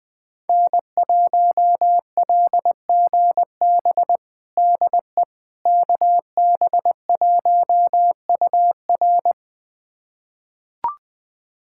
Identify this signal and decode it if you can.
Morse code, heard as a keyed tone, about 21 wpm, 700 Hz. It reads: N1LGB DE KB1UR
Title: USA Callsign DE Callsign - 5 Characters - 20wpm